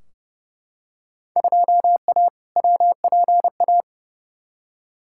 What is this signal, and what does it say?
Morse code, heard as a keyed tone, about 30 wpm, 700 Hz. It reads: 2A WPA
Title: Field Day Exchanges - 30wpm